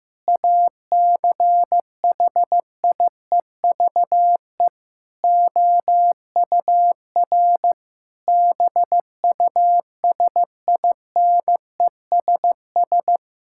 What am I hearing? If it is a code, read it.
Morse code, heard as a keyed tone, about 15 wpm, 700 Hz. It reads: ACHIEVE OUR BUSINESS
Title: Sentences from Top 1000 Words - Review Part 6 - 15wpm